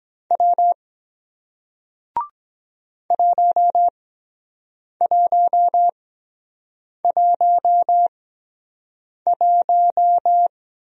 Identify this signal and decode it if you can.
Morse code, heard as a keyed tone, about 23 wpm, spaced as for 7 wpm, 700 Hz. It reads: W 1111